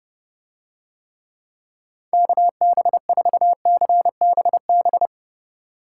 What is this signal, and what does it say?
Morse code, heard as a keyed tone, about 29 wpm, 700 Hz. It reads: KB4CBB